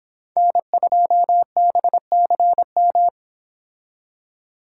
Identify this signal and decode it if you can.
Morse code, heard as a keyed tone, about 30 wpm, 700 Hz. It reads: N2BCM